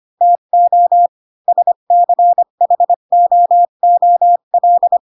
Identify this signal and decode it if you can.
Morse code, heard as a keyed tone, about 25 wpm, 700 Hz. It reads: TO SCHOOL
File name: Pinocchio / Pinocchio ch8 20 25 ik1hge